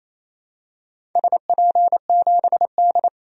Morse code, heard as a keyed tone, about 28 wpm, 700 Hz. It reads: SP7D
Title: DXCC-100 Call Signs - 4 Characters - 28wpm